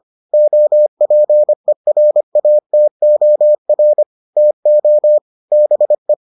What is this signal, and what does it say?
Morse code, heard as a keyed tone, about 25 wpm, 600 Hz. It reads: OPERATOR TO BE